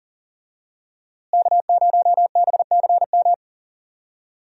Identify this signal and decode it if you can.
Morse code, heard as a keyed tone, about 40 wpm, 700 Hz. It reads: K0BCM
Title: USA Call Signs - 5 Characters - Repeated 2x - 40wpm